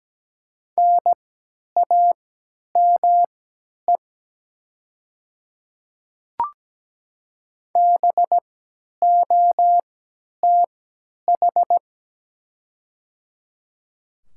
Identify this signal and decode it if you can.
Morse code, heard as a keyed tone, about 17 wpm, spaced as for 10 wpm, 690 Hz. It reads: NAME BOTH
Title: 3x Character Spacing - Top 300 Words - 17wpm